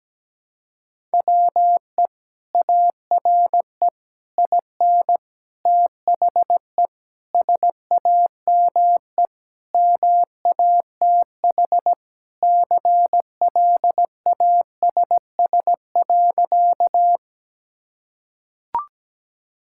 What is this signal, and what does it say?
Morse code, heard as a keyed tone, about 17 wpm, 700 Hz. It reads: WE ARE IN THE SAME MATH CLASS.
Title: Sentences from Top 1000 Words - Review Part 6 - 17wpm